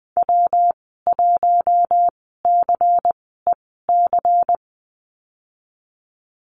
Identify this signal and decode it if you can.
Morse code, heard as a keyed tone, about 20 wpm, spaced as for 15 wpm, 700 Hz. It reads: W1CEC
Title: 179 Lesson 29g - Slash - Call Signs 20wpm